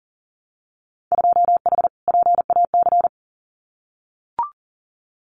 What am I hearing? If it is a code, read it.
Morse code, heard as a keyed tone, about 41 wpm, 700 Hz. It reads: 2H PAC
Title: Winter Field Day Exchanges Encore 40wpm